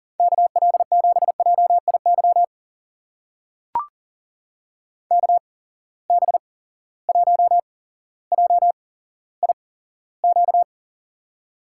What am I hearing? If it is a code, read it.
Morse code, heard as a keyed tone, about 40 wpm, spaced as for 17 wpm, 700 Hz. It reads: KL7JIY KB1JIQ
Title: USA Call Signs - 6 Characters - 8x Character Spacing - 40wpm